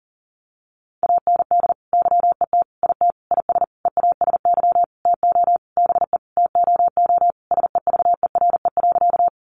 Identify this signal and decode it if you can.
Morse code, heard as a keyed tone, about 40 wpm, 700 Hz. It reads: AND YET IT IS EASY TO BE TOO SEVERE.
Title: Sentences from Top 1400 Words 40wpm